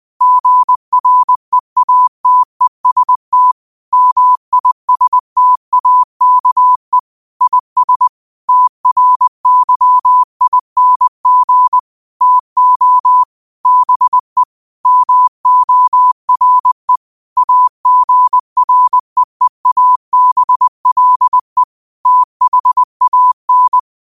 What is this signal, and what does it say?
Morse code, heard as a keyed tone, about 20 wpm, 1000 Hz. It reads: GREATEST MISTAKE IS TRYING TO BE MORE AGREEABLE THAN
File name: qotdfiles / 20wpm